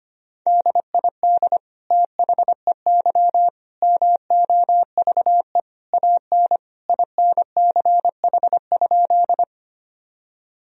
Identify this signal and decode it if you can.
Morse code, heard as a keyed tone, about 25 wpm, 700 Hz. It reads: DID THEY MOVE AN INCH?
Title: Sentences from Top 1000 Words - Review Part 3 - 25wpm